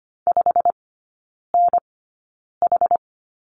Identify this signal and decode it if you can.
Morse code, heard as a keyed tone, about 25 wpm, spaced as for 9 wpm, 700 Hz. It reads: 5NH
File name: course / 25:10wpm / 093 Lesson 16e - Letter c - Three Characters 25wpm